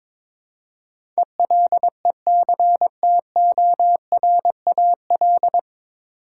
Morse code, heard as a keyed tone, about 22 wpm, 700 Hz. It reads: ELECTORAL
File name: 9 Letter Words - First 3 Letters Given - 22wpm